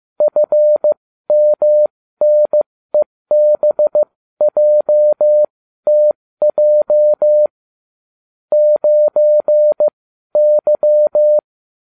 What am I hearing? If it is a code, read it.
Morse code, heard as a keyed tone, about 15 wpm, spaced as for 13 wpm, 600 Hz. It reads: FMNEBJTJ 9Y